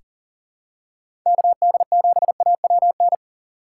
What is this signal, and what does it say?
Morse code, heard as a keyed tone, about 40 wpm, 700 Hz. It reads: KD7AWN
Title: USA Call Signs - 6 Characters - 2x Character Spacing - 40wpm